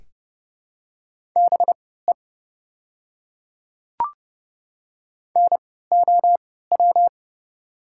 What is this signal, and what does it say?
Morse code, heard as a keyed tone, about 30 wpm, spaced as for 17 wpm, 700 Hz. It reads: BE NOW
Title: 3x Character Spacing - Top 100 Words - 30wpm